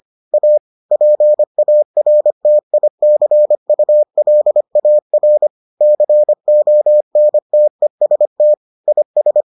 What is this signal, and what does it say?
Morse code, heard as a keyed tone, about 25 wpm, 600 Hz. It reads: A PARTICULAR CONTEST IS